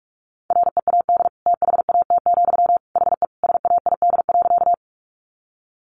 Morse code, heard as a keyed tone, about 44 wpm, 700 Hz. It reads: READ THAT, HE SAID.
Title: Sentences from Top 200 Words 45wpm